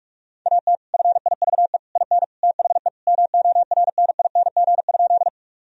Morse code, heard as a keyed tone, about 43 wpm, 700 Hz. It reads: AT FIVE IN THE MORNING?
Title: Sentences from Top 1000 Words - Review Part 2 - 45wpm